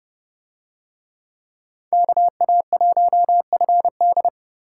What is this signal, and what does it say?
Morse code, heard as a keyed tone, about 28 wpm, 700 Hz. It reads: KA1FD